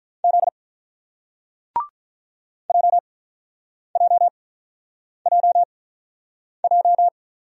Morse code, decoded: Z JJJJ

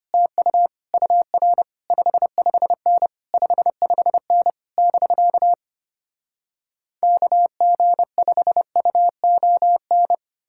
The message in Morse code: TU UR 55N 55N <BK> KG5UON

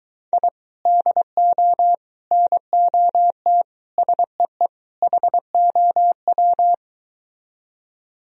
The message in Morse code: I DO NOT SEE HOW

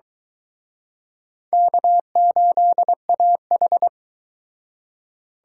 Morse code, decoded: K8AH

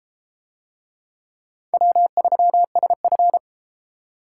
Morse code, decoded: W3SF

